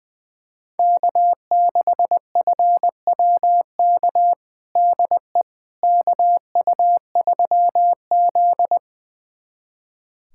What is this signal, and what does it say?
Morse code, heard as a keyed tone, about 21 wpm, 700 Hz. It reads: K6FWK DE KU3Z
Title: USA Callsign DE Callsign 20wpm